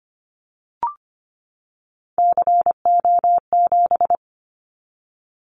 Morse code, heard as a keyed tone, about 25 wpm, 700 Hz. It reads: CO7
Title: Callsign Prefixes for Top 100 DXCC - 25wpm